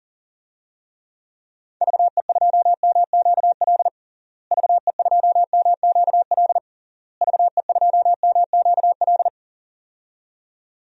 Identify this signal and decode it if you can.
Morse code, heard as a keyed tone, about 40 wpm, 700 Hz. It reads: VE2MQL VE2MQL VE2MQL
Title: DXCC-100 Call Signs - 6 Characters - Repeated 3x - 40wpm